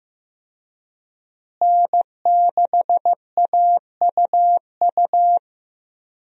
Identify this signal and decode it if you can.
Morse code, heard as a keyed tone, about 15 wpm, 700 Hz. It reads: N6AUU
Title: USA Call Signs - 5 Characters - Repeated 3x - 15wpm